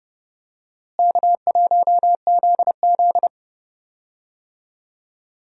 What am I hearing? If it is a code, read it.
Morse code, heard as a keyed tone, about 30 wpm, 700 Hz. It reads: K1ZZ
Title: USA Call Signs - 4 Characters - Repeated 2x - 30wpm